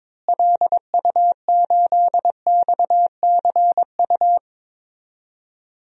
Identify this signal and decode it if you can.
Morse code, heard as a keyed tone, about 22 wpm, 700 Hz. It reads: LU8XCU
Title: DXCC-100 Call Signs - 6 Characters - Repeated 2x - 22wpm